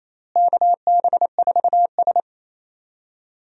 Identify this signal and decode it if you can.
Morse code, heard as a keyed tone, about 28 wpm, 700 Hz. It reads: KB4S
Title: USA Call Signs - 4 Characters - Repeated 2x - 28wpm